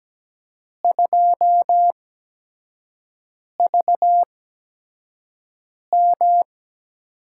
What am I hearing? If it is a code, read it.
Morse code, heard as a keyed tone, about 17 wpm, spaced as for 5 wpm, 700 Hz. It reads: 2VM
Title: USA Call Signs - 4 Characters - 8x Character Spacing - 17wpm